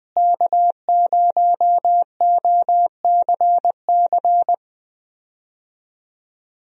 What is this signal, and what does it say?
Morse code, heard as a keyed tone, about 20 wpm, 700 Hz. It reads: K0OCC
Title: USA Call Signs - 5 Characters - Repeated 2x - 20wpm